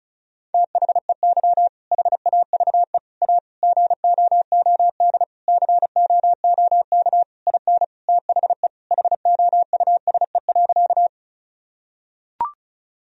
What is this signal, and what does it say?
Morse code, heard as a keyed tone, about 36 wpm, 700 Hz. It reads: THEY HAVE A GOOD COOK IN THE HOUSE.
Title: Sentences from Top 1000 Words - Review Part 4 - 35wpm